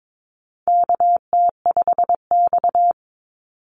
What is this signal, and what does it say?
Morse code, heard as a keyed tone, about 22 wpm, 700 Hz. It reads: KT5X
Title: NA SOTA Activator Callsigns 22wpm